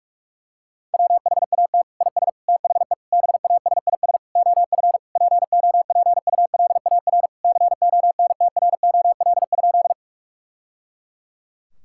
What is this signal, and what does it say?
Morse code, heard as a keyed tone, about 46 wpm, 700 Hz. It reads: WHAT IS THE BASIS OF POPULAR CONTROL?
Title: Sentences from Top 1000 Words - Review Part 5 - 45wpm